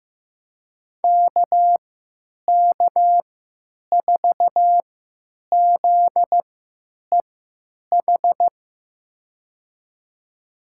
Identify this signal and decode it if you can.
Morse code, heard as a keyed tone, about 16 wpm, spaced as for 9 wpm, 700 Hz. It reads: KK4ZEH
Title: USA Call Signs - 6 Characters - 3x Character Spacing - 15wpm